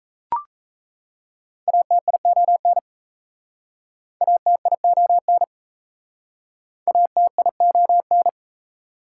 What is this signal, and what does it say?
Morse code, heard as a keyed tone, about 37 wpm, 690 Hz. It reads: ATION ATION ATION